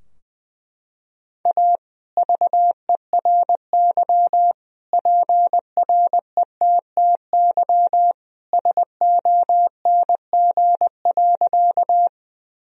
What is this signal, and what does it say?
Morse code, heard as a keyed tone, about 21 wpm, 700 Hz. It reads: A VERY PRETTY SONG.
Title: Sentences from Top 1000 Words - Review Part 2 - 20wpm